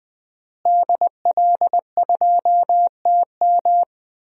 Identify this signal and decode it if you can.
Morse code, heard as a keyed tone, about 20 wpm, 700 Hz. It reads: DL2TM
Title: DXCC-100 Call Signs - 5 Characters - Repeated 3x - 20wpm